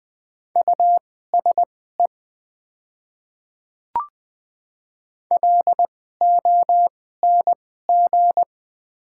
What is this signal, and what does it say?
Morse code, heard as a keyed tone, about 20 wpm, spaced as for 14 wpm, 700 Hz. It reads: USE LONG
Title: 2x Character Spacing - Top 100 Words - 20wpm